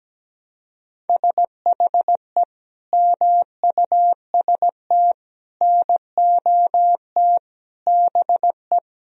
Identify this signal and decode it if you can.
Morse code, heard as a keyed tone, about 17 wpm, 700 Hz. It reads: SHE MUST NOT BE
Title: Sentences from Top 1000 Words - Review Part 1 - 17wpm